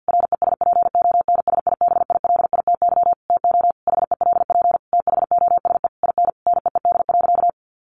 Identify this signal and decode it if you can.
Morse code, heard as a keyed tone, about 50 wpm, 700 Hz. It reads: RESPONSIBILITY TO HELP THOSE IN NEED.
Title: Sentences from Top 1200 Words 50wpm